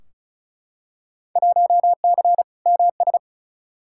Morse code, heard as a keyed tone, about 36 wpm, 700 Hz. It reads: 1C MS
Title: Field Day Exchanges Encore - 35wpm